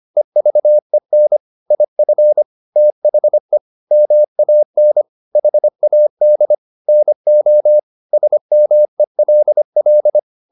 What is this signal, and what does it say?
Morse code, heard as a keyed tone, about 25 wpm, 600 Hz. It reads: EVEN IF THE MAN HAD NO SMELL